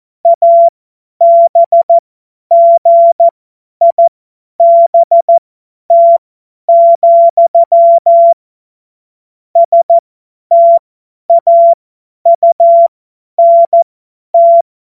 Morse code, heard as a keyed tone, about 14 wpm, spaced as for 10 wpm, 680 Hz. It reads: ABGIBT, STAUNT
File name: Morsekurs / Witze / Witz 13. September 2022 50 F70